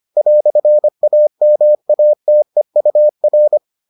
Morse code, heard as a keyed tone, about 25 wpm, 600 Hz. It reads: "AMATEUR